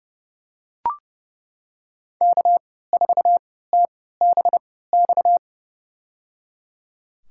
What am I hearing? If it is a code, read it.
Morse code, heard as a keyed tone, about 29 wpm, spaced as for 17 wpm, 700 Hz. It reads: K4TBX
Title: USA Call Signs - 3x Character Spacing - 30wpm